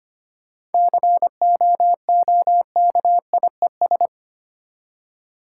Morse code, heard as a keyed tone, about 25 wpm, 700 Hz. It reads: COOKIES
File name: practice / 25wpm / Easy Words with Hints - 25wpm